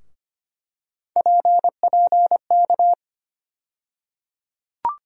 Morse code, heard as a keyed tone, about 25 wpm, 700 Hz. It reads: PPK
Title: USA Call Sign Postfixes - 3 Letters 25wpm